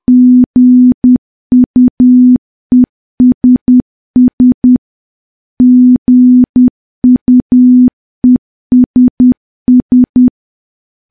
Morse code, heard as a keyed tone, about 10 wpm, 250 Hz. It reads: GUESS GUESS
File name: morse / morse-training-10WPM